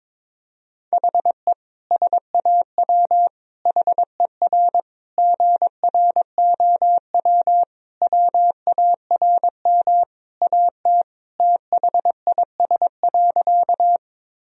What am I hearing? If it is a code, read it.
Morse code, heard as a keyed tone, about 22 wpm, 700 Hz. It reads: HE SAW HER GROW WARM AT THIS.
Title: Sentences from Top 1000 Words - Review Part 2 - 22wpm